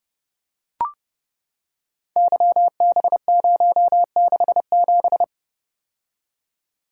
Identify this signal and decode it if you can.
Morse code, heard as a keyed tone, about 30 wpm, 700 Hz. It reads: YB067
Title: Code Groups - 5 Letter:Numbers - 30wpm